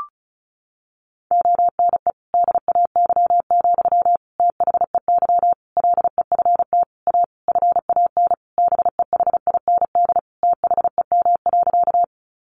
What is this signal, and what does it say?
Morse code, heard as a keyed tone, about 34 wpm, 700 Hz. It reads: ONE DAY, THEY LEFT A FAN BEHIND THEM.
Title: Sentences from Top 1300 Words 35wpm